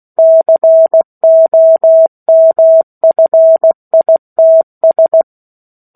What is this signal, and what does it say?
Morse code, heard as a keyed tone, about 16 wpm, 650 Hz. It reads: COMFITS